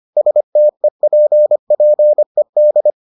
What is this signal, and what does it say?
Morse code, heard as a keyed tone, about 25 wpm, 600 Hz. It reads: STEPPED